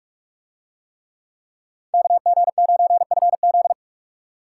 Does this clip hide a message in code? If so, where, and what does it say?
Yes: Morse code from 1.9 to 3.7 s, heard as a keyed tone, about 44 wpm, 700 Hz. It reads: KG9FZ